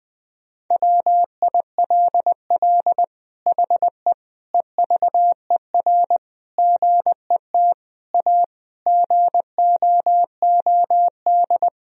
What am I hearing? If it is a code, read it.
Morse code, heard as a keyed tone, about 20 wpm, 700 Hz. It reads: WILL HE EVER GET A GOOD
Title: Sentences from Top 1000 Words - Review Part 3 - 20wpm